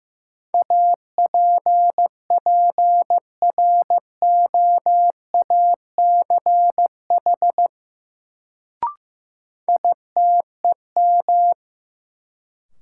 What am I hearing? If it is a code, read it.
Morse code, heard as a keyed tone, about 15 wpm, 700 Hz. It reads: APPROACH ITEM
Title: Sentences from Top 1400 Words - New Word Drill Alt - 15wpm